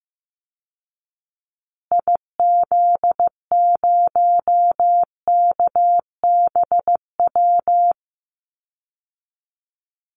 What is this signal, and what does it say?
Morse code, heard as a keyed tone, about 15 wpm, 700 Hz. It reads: IZ0KBW
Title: Call Signs Repeated 2x - 15wpm